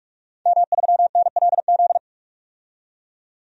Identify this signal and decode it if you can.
Morse code, heard as a keyed tone, about 43 wpm, 700 Hz. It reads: M3NLZ
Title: DXCC-100 Call Signs - 5 Characters - 45wpm